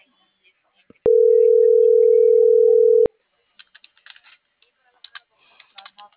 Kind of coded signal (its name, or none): telephone call progress tone